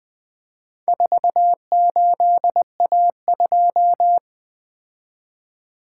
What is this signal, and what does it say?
Morse code, heard as a keyed tone, about 24 wpm, 700 Hz. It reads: 48A2
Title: Speed-Racing - Four Letter-Number Combinations - 20wpm